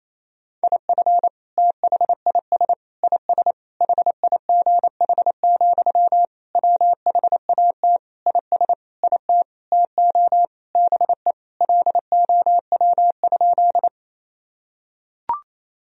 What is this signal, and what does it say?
Morse code, heard as a keyed tone, about 28 wpm, 700 Hz. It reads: IF THIS IS HIGH, WHAT IS IT TO BE LOW?
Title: Sentences from Top 1000 Words - Review Part 1 - 28wpm